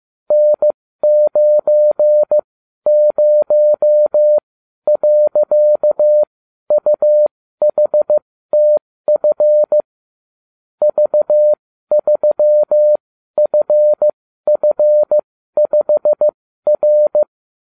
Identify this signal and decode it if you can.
Morse code, heard as a keyed tone, about 15 wpm, spaced as for 12 wpm, 600 Hz. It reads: N90.UHTF V3FF5R